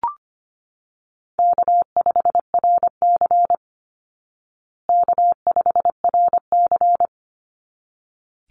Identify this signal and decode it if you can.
Morse code, heard as a keyed tone, about 25 wpm, 700 Hz. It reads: K5RC K5RC
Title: Call Signs Repeated 3x - 25wpm